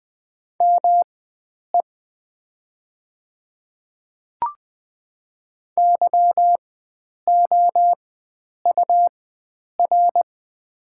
Morse code, heard as a keyed tone, about 20 wpm, spaced as for 9 wpm, 700 Hz. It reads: ME YOUR
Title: 4x Character Spacing - Top 300 Words - 20wpm